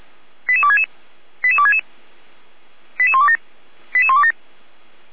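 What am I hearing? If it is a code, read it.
Five-tone selective call (ZVEI), heard as quick runs of five tones, tones of about 70 ms. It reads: 88266 88266 88127 88127
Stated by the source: ZVEI five-tone call 88267 received by SDR.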